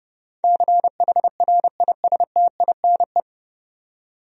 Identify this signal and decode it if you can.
Morse code, heard as a keyed tone, about 30 wpm, 700 Hz. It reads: CHRISTINE
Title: USA Common Names - 30wpm